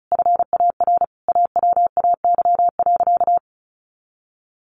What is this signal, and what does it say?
Morse code, heard as a keyed tone, about 34 wpm, 700 Hz. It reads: FAR AWAY.
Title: Additional Easy Phrases 35wpm